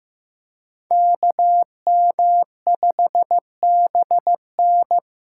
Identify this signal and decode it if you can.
Morse code, heard as a keyed tone, about 15 wpm, 700 Hz. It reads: KM5BN
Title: USA Callsign DE Callsign - 5 Characters - 15wpm